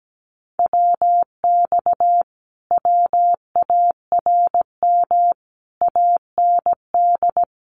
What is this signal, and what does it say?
Morse code, heard as a keyed tone, about 17 wpm, 700 Hz. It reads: WX WARM AND
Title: Common QSO Phrases Encore 17wpm